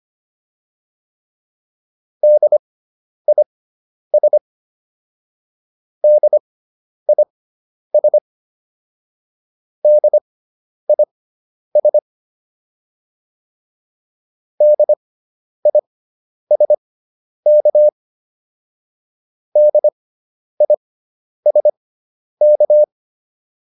Morse code, heard as a keyed tone, about 25 wpm, spaced as for 10 wpm, 600 Hz. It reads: DIS DIS DIS DISK DISK